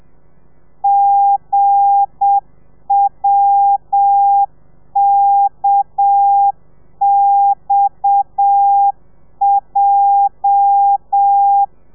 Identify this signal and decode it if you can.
Morse code, heard as a keyed tone, about 7 wpm, 800 Hz. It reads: GWKXJ
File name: telegrafia / i2tfj / Lez8 35